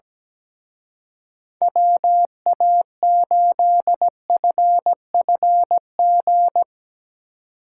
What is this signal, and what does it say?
Morse code, heard as a keyed tone, about 18 wpm, 700 Hz. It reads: WA8FFG
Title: Speed Racing - USA Call Signs 17wpm